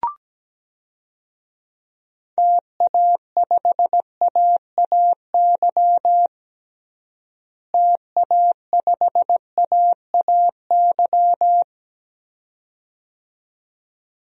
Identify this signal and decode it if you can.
Morse code, heard as a keyed tone, about 17 wpm, 700 Hz. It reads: TA5AAY TA5AAY
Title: DXCC-100 Call Signs - 6 Characters - Repeated 2x - 17wpm